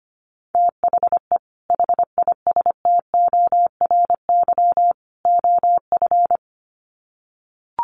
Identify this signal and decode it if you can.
Morse code, heard as a keyed tone, about 25 wpm, 700 Hz. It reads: THE HISTORY OF
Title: Sets of 3 Words 25wpm